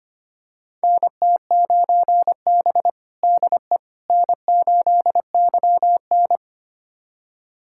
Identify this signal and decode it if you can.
Morse code, heard as a keyed tone, about 25 wpm, 700 Hz. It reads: NT9B DE N8YN
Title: USA Callsign DE Callsign - 4 Characters - 25wpm